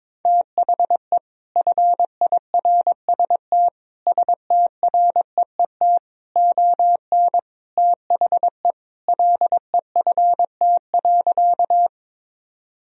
Morse code, heard as a keyed tone, about 21 wpm, 700 Hz. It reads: THE FIRST STREET ON THE LEFT.
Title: Sentences from Top 1000 Words - Review Part 3 - 22wpm